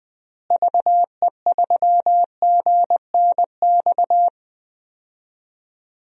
Morse code, heard as a keyed tone, about 20 wpm, 700 Hz. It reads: VE3GNX
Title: DXCC-100 Call Signs - 6 Characters - Repeated 2x - 20wpm